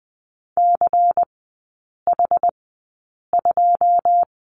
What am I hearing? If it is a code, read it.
Morse code, heard as a keyed tone, about 20 wpm, spaced as for 8 wpm, 700 Hz. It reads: CH2
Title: 093 Lesson 16e - Letter c - Three Characters 20wpm